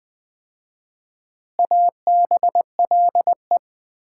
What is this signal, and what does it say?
Morse code, heard as a keyed tone, about 20 wpm, 700 Hz. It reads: ABLE
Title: 4x Word-Spacing - Sets of 4 Words - 20wpm